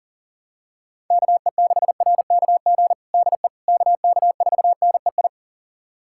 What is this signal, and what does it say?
Morse code, heard as a keyed tone, about 40 wpm, 700 Hz. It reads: KE6RKG DE KK4NEI
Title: USA Callsign DE Callsign 40wpm